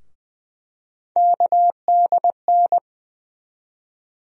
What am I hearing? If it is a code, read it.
Morse code, heard as a keyed tone, about 20 wpm, 700 Hz. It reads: KDN
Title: USA Call Sign Postfixes - 3 Letters 20wpm